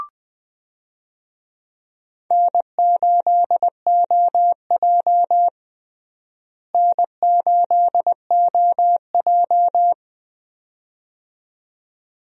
Morse code, heard as a keyed tone, about 20 wpm, 700 Hz. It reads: N8OJ N8OJ
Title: USA Call Signs - 4 Characters - Repeated 2x - 20wpm